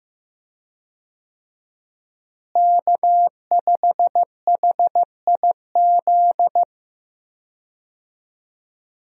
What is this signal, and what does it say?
Morse code, heard as a keyed tone, about 15 wpm, 700 Hz. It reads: K5HIZ
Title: USA Call Signs - 5 Characters - Repeated 2x - 15wpm